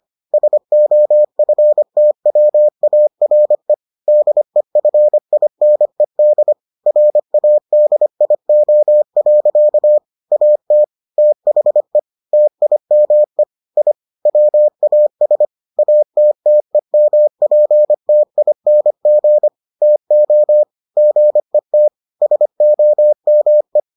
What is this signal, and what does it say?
Morse code, heard as a keyed tone, about 25 wpm, 600 Hz. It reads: SOFTWARE DEFINED RADIO. AT THE TIME I WAS ATTEMPTING TO GET SOME